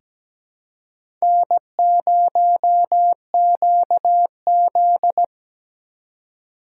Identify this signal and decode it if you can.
Morse code, heard as a keyed tone, about 17 wpm, 700 Hz. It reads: N0QZ
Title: USA Call Signs - 4 Characters - 2x Character Spacing - 17wpm